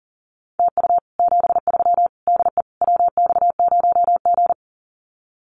Morse code, heard as a keyed tone, about 40 wpm, 700 Hz. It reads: TU 73 DE WX0G
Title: Common QSO Phrases Encore 40wpm